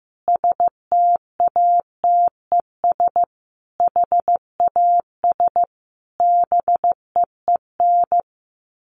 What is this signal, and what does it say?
Morse code, heard as a keyed tone, about 15 wpm, 700 Hz. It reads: STATES HAS BEEN
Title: Sets of 5 Words Encore2 15wpm